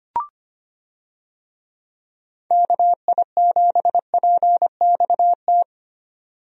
Morse code, heard as a keyed tone, about 25 wpm, 700 Hz. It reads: KI7PXT
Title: USA Call Signs - 6 Characters - 25wpm